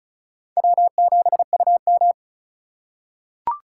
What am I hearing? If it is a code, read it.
Morse code, heard as a keyed tone, about 36 wpm, 700 Hz. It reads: W7UM
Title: USA Call Signs - 4 Characters - 4x Character Spacing - 35wpm